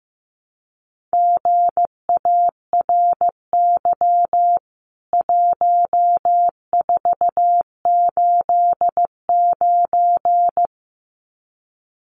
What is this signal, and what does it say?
Morse code, heard as a keyed tone, about 15 wpm, 700 Hz. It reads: GARY 1489
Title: CWT Exchange 15wpm